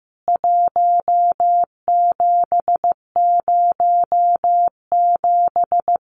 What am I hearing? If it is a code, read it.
Morse code, heard as a keyed tone, about 15 wpm, 700 Hz. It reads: 1707
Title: CWT Exchange 15wpm